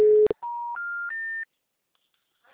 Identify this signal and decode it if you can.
Special information tone.
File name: out- -2041- - - .61244